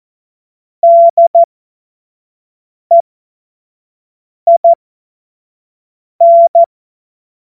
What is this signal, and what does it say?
Morse code, heard as a keyed tone, about 14 wpm, spaced as for 5 wpm, 680 Hz. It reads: DEIN
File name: Morsekurs / Witze / Witz 20. Dezember 2022 25 F70